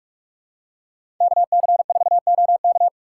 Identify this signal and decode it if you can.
Morse code, heard as a keyed tone, about 47 wpm, 700 Hz. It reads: KC4OK
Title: USA Call Signs - 5 Characters - 6x Character Spacing - 45wpm